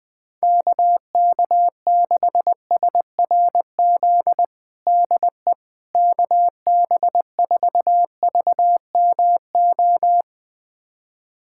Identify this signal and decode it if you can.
Morse code, heard as a keyed tone, about 21 wpm, 700 Hz. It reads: KK6SRZ DE KB4VMO
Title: USA Callsign DE Callsign - 6 Characters - 20wpm